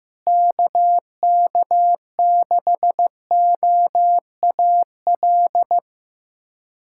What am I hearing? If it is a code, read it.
Morse code, heard as a keyed tone, about 15 wpm, 700 Hz. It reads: KK6OAL
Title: USA Callsign DE Callsign - 6 Characters - 15wpm